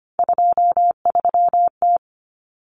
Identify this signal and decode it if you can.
Morse code, heard as a keyed tone, about 25 wpm, 700 Hz. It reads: 23T
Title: Common Cut Numbers - 3-Digit 25wpm